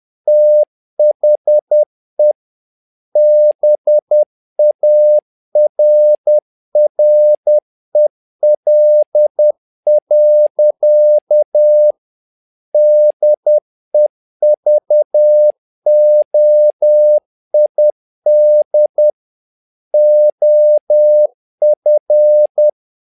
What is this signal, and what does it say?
Morse code, heard as a keyed tone, about 10 wpm, 600 Hz. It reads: THE BARREL. DEVOID OF
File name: learning center / tower trouble / tower trouble 10wpm